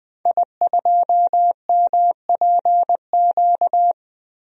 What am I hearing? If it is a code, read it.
Morse code, heard as a keyed tone, about 20 wpm, 700 Hz. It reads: I2MPQ
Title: DXCC-100 Call Signs - 5 Characters - 20wpm